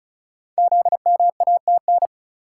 Morse code, heard as a keyed tone, about 36 wpm, 700 Hz. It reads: ZMATN
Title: Code Groups - 5 Letters - 35wpm